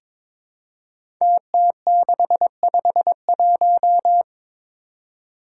Morse code, heard as a keyed tone, about 22 wpm, 700 Hz. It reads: TT651
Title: Five Challenging Letter-Number Combinations 22wpm